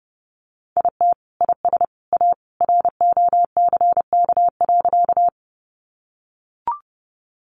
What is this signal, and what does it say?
Morse code, heard as a keyed tone, about 30 wpm, 700 Hz. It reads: IT IS A ROCK.
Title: Sentences from Top 400 Words 30wpm